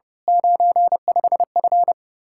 Morse code, heard as a keyed tone, about 28 wpm, 700 Hz. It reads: 95F
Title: Speed-Racing Common QSO Phrases - 30wpm